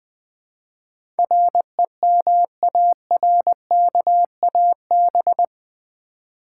Morse code, heard as a keyed tone, about 20 wpm, 700 Hz. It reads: REMARKAB
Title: Build A Word - 4 Letters At A Time - 10-Letter Words - 20wpm